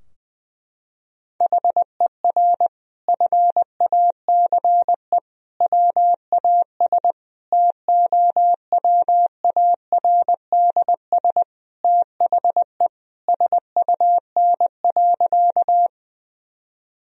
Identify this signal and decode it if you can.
Morse code, heard as a keyed tone, about 19 wpm, 700 Hz. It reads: HER FACE WAS TOWARDS THE SUN.
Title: Sentences from Top 1000 Words - Review Part 2 - 20wpm